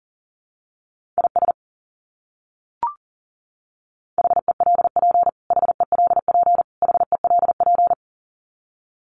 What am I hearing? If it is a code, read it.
Morse code, heard as a keyed tone, about 40 wpm, 700 Hz. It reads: IS HELP HELP HELP
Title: Top 200 Words Repeated 40wpm